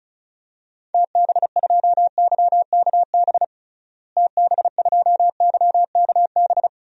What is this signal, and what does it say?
Morse code, heard as a keyed tone, about 36 wpm, 700 Hz. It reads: TB2YKB TB2YKB
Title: DXCC-100 Call Signs - 6 Characters - Repeated 2x - 35wpm